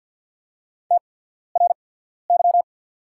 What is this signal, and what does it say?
Morse code, heard as a keyed tone, about 48 wpm, spaced as for 14 wpm, 690 Hz. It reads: TRY
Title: 8x Character Spacing - Top 300 Words - 50wpm